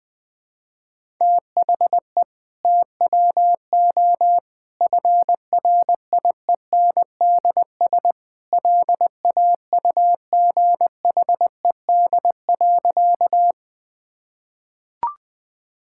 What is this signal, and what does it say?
Morse code, heard as a keyed tone, about 21 wpm, 700 Hz. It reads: THE TWO FRIENDS LAUGHED.
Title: Sentences with 2-Word Phrases - 20wpm